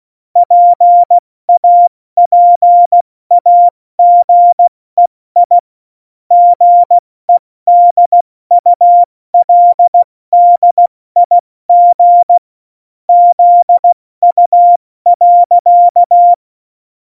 Morse code, heard as a keyed tone, about 16 wpm, spaced as for 14 wpm, 700 Hz. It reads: PAPAGEI GEDULDIG ZU.
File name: Morsekurs / Witze / Witz 15. August 17, Tempo 70 F80